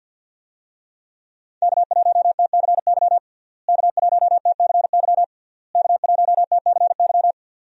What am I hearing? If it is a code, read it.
Morse code, heard as a keyed tone, about 52 wpm, 700 Hz. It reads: K1TCY K1TCY K1TCY
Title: USA Call Signs - 5 Characters - Repeated 3x - 50wpm